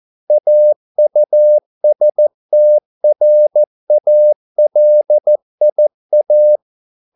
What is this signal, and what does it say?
Morse code, heard as a keyed tone, about 14 wpm, 600 Hz. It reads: AUSTRALIA